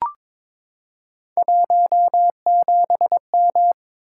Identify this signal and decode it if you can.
Morse code, heard as a keyed tone, about 22 wpm, 700 Hz. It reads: 17M
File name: practice / 22wpm / Three Letter-Number Combinations - 22wpm